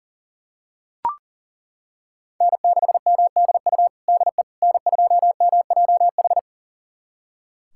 Morse code, heard as a keyed tone, about 40 wpm, 700 Hz. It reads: N6MDU DE N2MJH
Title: USA Callsign DE Callsign - 5 Characters - 40wpm